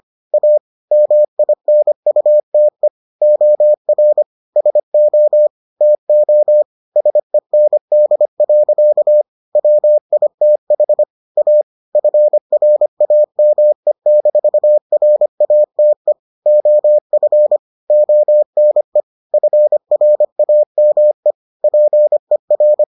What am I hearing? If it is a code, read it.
Morse code, heard as a keyed tone, about 25 wpm, 600 Hz. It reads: A MINUTE OR SO TO SEND. WITH A FRAME-RATE OF ONE FRAME PER